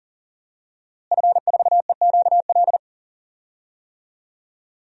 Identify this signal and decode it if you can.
Morse code, heard as a keyed tone, about 40 wpm, 700 Hz. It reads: F4EQL
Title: DXCC-100 Call Signs - 5 Characters - Repeated 3x - 40wpm